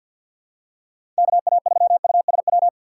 Morse code, heard as a keyed tone, about 52 wpm, 700 Hz. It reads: KA3USW
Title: USA Call Signs - 6 Characters - 6x Character Spacing - 50wpm